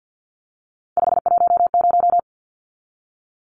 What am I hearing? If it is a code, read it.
Morse code, heard as a keyed tone, about 50 wpm, 700 Hz. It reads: 510
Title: Numbers - 3-Digit 50wpm